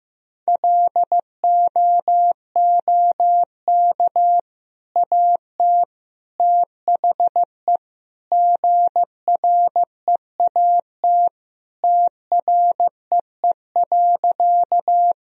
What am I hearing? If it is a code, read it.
Morse code, heard as a keyed tone, about 15 wpm, 700 Hz. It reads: LOOK AT THE GREAT TREE.
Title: Sentences from Top 1000 Words - Review Part 1 - 15wpm